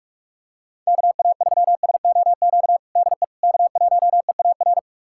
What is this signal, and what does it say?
Morse code, heard as a keyed tone, about 45 wpm, 700 Hz. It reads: KA3SOQ DE K1EAR
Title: USA Callsign DE Callsign 45wpm